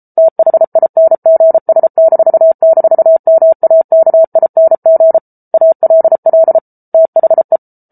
Morse code, heard as a keyed tone, about 34 wpm, 650 Hz. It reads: THINGS--MAKING ALL THE